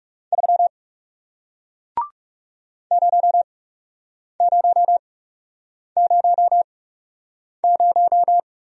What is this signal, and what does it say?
Morse code, heard as a keyed tone, about 40 wpm, spaced as for 8 wpm, 700 Hz. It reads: 3 0000